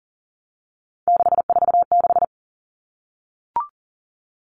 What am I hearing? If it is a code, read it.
Morse code, heard as a keyed tone, about 40 wpm, 700 Hz. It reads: 646
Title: All Cut Numbers - 3-Digit 40wpm